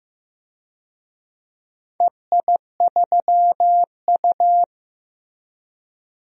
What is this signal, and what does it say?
Morse code, heard as a keyed tone, about 15 wpm, 700 Hz. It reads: EI3U
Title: DXCC-100 Call Signs - 4 Characters - 15wpm